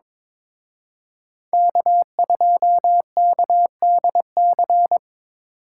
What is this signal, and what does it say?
Morse code, heard as a keyed tone, about 25 wpm, 700 Hz. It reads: K2KDC
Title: Speed Racing - USA 1x3 Call Signs 22wpm